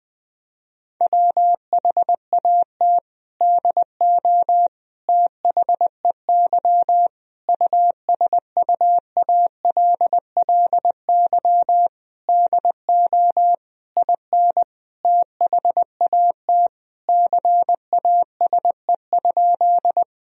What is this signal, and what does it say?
Morse code, heard as a keyed tone, about 21 wpm, 700 Hz. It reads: WHAT DO THEY USUALLY DO IN THAT CASE?
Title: Sentences from Top 1000 Words - Review Part 3 - 20wpm